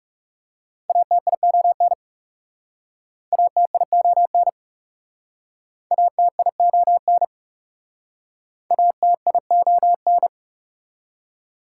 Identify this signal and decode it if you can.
Morse code, heard as a keyed tone, about 37 wpm, 700 Hz. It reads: ATION ATION ATION ATION
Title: Speed-Racing Prefixes and Suffixes 30wpm